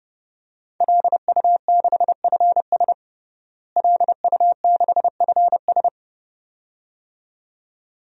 Morse code, heard as a keyed tone, about 30 wpm, 700 Hz. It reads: LU6FS LU6FS
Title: DXCC-100 Call Signs - 5 Characters - Repeated 2x - 30wpm